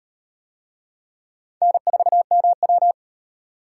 Morse code, heard as a keyed tone, about 38 wpm, 700 Hz. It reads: N4MW